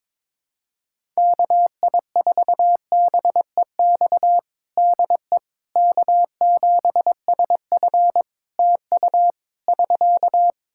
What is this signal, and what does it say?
Morse code, heard as a keyed tone, about 22 wpm, 700 Hz. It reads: KI4BEX DE K7SF TU <SK>
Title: Practice QSO - Exchange 4 - Ending - 22wpm